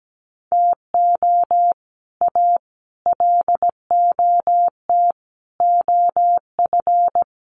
Morse code, heard as a keyed tone, about 17 wpm, 700 Hz. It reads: TO A LOT OF
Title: Sets of 5 Words Encore 17wpm